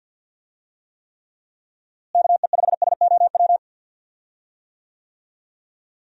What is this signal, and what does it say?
Morse code, heard as a keyed tone, about 51 wpm, 700 Hz. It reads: KE5SOW
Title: USA Call Signs - 6 Characters - Repeated 3x - 50wpm